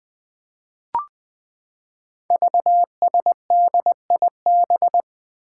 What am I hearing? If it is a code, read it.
Morse code, heard as a keyed tone, about 20 wpm, 700 Hz. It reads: VSDIB
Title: Five Challenging Letter-Number Combinations 20wpm